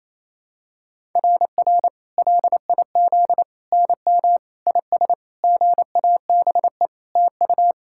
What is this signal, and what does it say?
Morse code, heard as a keyed tone, about 28 wpm, 700 Hz. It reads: RR LIZ NM IS GABE TU
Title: SOTA QSO Practice 28wpm